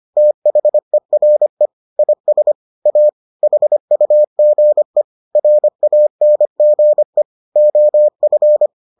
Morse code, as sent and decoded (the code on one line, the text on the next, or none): - .... .-. / .. ... / .- / .... ..- --. / .-. .- -. --. / --- ..-.
THERE IS A HUGE RANGE OF